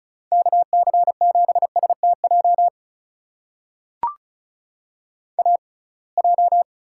Morse code, heard as a keyed tone, about 34 wpm, spaced as for 26 wpm, 700 Hz. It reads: KC7STJ AJ
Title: USA Call Signs - 6x Character Spacing - 35wpm